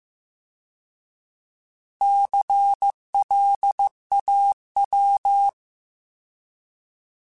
Morse code, heard as a keyed tone, about 15 wpm, 800 Hz. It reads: CLAW